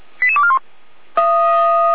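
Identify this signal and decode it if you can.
Five-tone selective call (ZVEI), heard as quick runs of five tones, tones of about 70 ms. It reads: 88241
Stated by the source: Five-tone ZVEI selective call 88402, received by SDR, fire brigade Orlamünde (feuerwehr-orlamuende).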